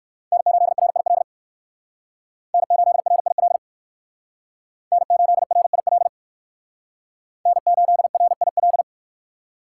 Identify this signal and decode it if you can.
Morse code, heard as a keyed tone, about 61 wpm, 700 Hz. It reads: N8RIL N8RIL N8RIL N8RIL